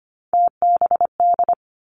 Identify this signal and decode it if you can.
Morse code, heard as a keyed tone, about 25 wpm, 700 Hz. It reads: TBD
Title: Three Challenging Letter-Number Combinations 25wpm